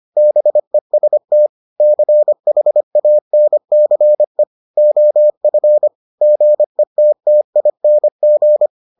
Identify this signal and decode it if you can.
Morse code, heard as a keyed tone, about 25 wpm, 600 Hz. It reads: BEST CHANCE OF GETTING